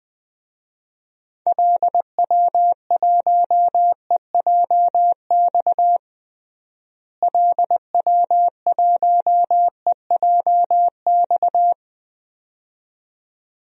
Morse code, heard as a keyed tone, about 20 wpm, 700 Hz. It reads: LW1EJX LW1EJX
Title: DXCC-100 Call Signs - 6 Characters - Repeated 2x - 20wpm